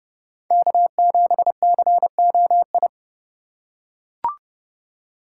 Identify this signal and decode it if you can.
Morse code, heard as a keyed tone, about 32 wpm, 700 Hz. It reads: K7COI
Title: USA Call Signs - 5 Characters - 3x Character Spacing - 30wpm